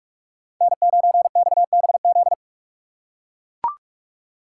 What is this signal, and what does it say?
Morse code, heard as a keyed tone, about 42 wpm, 700 Hz. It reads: N9XBZ
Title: USA Call Signs - 3x Character Spacing - 45wpm